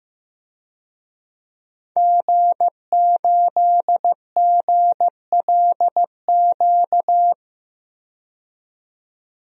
Morse code, heard as a keyed tone, about 15 wpm, 700 Hz. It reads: G8GLQ
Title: DXCC-100 Call Signs - 5 Characters - 15wpm